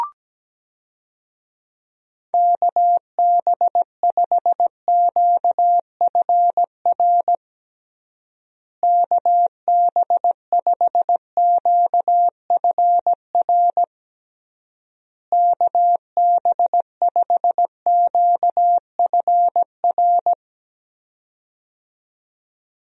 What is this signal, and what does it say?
Morse code, heard as a keyed tone, about 17 wpm, 700 Hz. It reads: KB5QFR KB5QFR KB5QFR
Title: USA Call Signs - 6 Characters - Repeated 3x - 17wpm